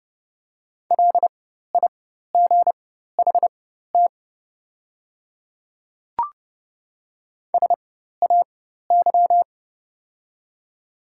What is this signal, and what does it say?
Morse code, heard as a keyed tone, about 32 wpm, spaced as for 14 wpm, 700 Hz. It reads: LIGHT SAY
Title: 4x Character Spacing - Top 300 Words - 30wpm